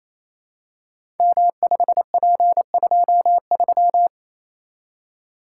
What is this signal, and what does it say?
Morse code, heard as a keyed tone, about 27 wpm, 700 Hz. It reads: M5P23
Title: Code Groups - 5 Letter:Numbers - 28wpm